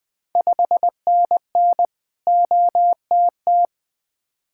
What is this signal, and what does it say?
Morse code, heard as a keyed tone, about 20 wpm, 700 Hz. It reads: 5NN OTT
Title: Ontario QSO Party Exchange 20wpm